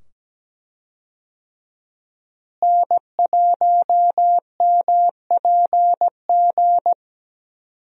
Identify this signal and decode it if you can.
Morse code, heard as a keyed tone, about 17 wpm, 700 Hz. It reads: N1MPG
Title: USA Call Signs - 5 Characters - Repeated 2x - 17wpm